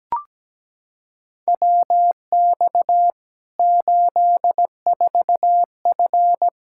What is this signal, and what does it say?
Morse code, heard as a keyed tone, about 17 wpm, 700 Hz. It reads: WX 84F
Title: K1USN SST QSO Practice 17wpm